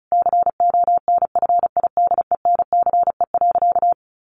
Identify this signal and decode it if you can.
Morse code, heard as a keyed tone, about 34 wpm, 700 Hz. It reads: CONFIDENCE.
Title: Sentences from Top 1500 Words 35wpm